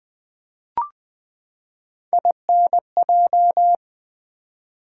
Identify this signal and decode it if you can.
Morse code, heard as a keyed tone, about 20 wpm, 700 Hz. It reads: INJ
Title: Build A Word - 3 Letters At A Time - 6-Letter Words - 20wpm